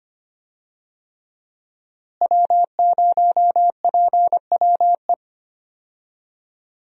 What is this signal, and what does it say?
Morse code, heard as a keyed tone, about 25 wpm, 700 Hz. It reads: W0PWE
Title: USA Call Signs - 5 Characters - Repeated 3x - 25wpm